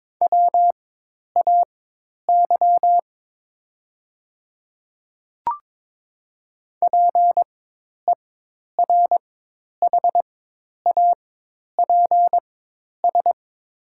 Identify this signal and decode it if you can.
Morse code, heard as a keyed tone, about 22 wpm, spaced as for 10 wpm, 700 Hz. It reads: WAY PERHAPS
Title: 4x Character Spacing - Top 500 Words - 22wpm